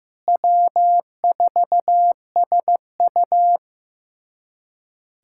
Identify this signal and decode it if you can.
Morse code, heard as a keyed tone, about 15 wpm, 700 Hz. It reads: W4SU
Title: USA Callsign DE Callsign - 4 Characters - 15wpm